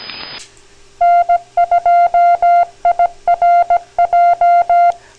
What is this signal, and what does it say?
Morse code, heard as a keyed tone, about 17 wpm, 700 Hz. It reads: N2IRJ
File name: recordings / WB4AIO - 3, N2IRJ, KB8UYJ 3995 kHz, LSB 2014 April 2nd circa 6 PM EST 2h 16m 18s Source N2IRJ